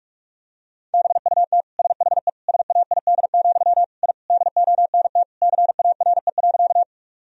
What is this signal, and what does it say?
Morse code, heard as a keyed tone, about 46 wpm, 700 Hz. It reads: BUT SHE SAID, I DONT CARE.
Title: Sentences from Top 1000 Words - Review Part 1 - 45wpm